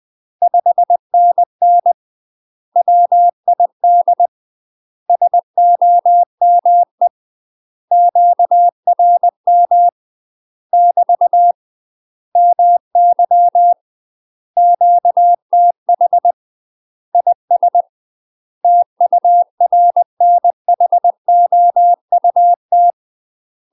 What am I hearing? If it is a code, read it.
Morse code, heard as a keyed tone, about 20 wpm, 700 Hz. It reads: 5NN WID SOME QRM <BT> MY QTH IS TURNHOUT
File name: UK MASTER QSOs TRAINING / 20WPM / UK MASTER QSOs 0055